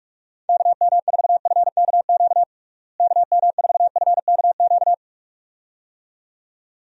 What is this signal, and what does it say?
Morse code, heard as a keyed tone, about 46 wpm, 700 Hz. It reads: KM4FKQ KM4FKQ
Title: USA Call Signs - 6 Characters - Repeated 2x - 45wpm